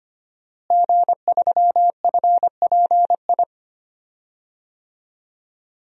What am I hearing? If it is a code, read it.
Morse code, heard as a keyed tone, about 25 wpm, 700 Hz. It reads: G3FPI
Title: DXCC-100 Call Signs - 5 Characters - Repeated 2x - 25wpm